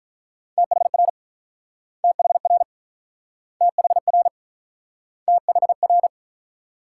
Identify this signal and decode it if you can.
Morse code, heard as a keyed tone, about 47 wpm, 700 Hz. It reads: THR THR THR THR